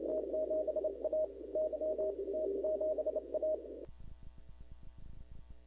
Morse code, heard as a keyed tone, about 28 wpm, 620 Hz, with noise. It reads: T7A Y T7A
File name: CW / 2005-k5tr arrl cw dx -06:00